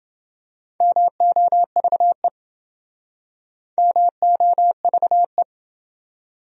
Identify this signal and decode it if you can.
Morse code, heard as a keyed tone, about 27 wpm, 700 Hz. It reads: MOVE MOVE